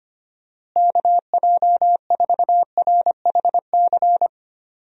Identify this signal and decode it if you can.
Morse code, heard as a keyed tone, about 25 wpm, 700 Hz. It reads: KJ4RHC
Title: USA Call Signs - 6 Characters - Repeated 2x - 25wpm